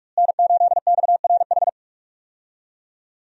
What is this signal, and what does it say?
Morse code, heard as a keyed tone, about 43 wpm, 700 Hz. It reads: N8XRH
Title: USA Call Signs - 5 Characters - Repeated 3x - 45wpm